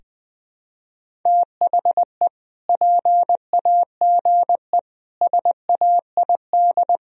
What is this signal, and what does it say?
Morse code, heard as a keyed tone, about 20 wpm, 700 Hz. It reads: THE PAGE SAID
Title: Sentences from Top 1000 Words - Review Part 1 - 20wpm